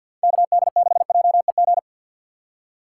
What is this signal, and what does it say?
Morse code, heard as a keyed tone, about 52 wpm, 700 Hz. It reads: K D 6 J E G
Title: USA Call Signs - 6 Characters - 6x Character Spacing - 50wpm